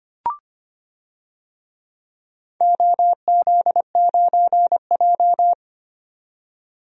Morse code, heard as a keyed tone, about 25 wpm, 700 Hz. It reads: OZ9J
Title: DXCC-100 Call Signs - 4 Characters - 25wpm